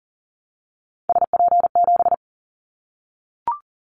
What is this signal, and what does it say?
Morse code, heard as a keyed tone, about 40 wpm, 700 Hz. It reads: SP7
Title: Callsign Prefixes for Top 25 DXCC - Repeated 3x - 40wpm